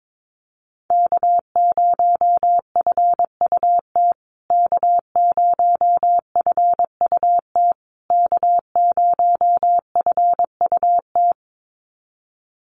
Morse code, heard as a keyed tone, about 22 wpm, 700 Hz. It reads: K0FUT K0FUT K0FUT
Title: USA Call Signs Repeated 3x 22wpm